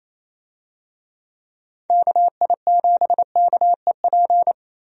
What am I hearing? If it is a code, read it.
Morse code, heard as a keyed tone, about 28 wpm, 700 Hz. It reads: KI7KEP
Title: USA Call Signs - 6 Characters - 28wpm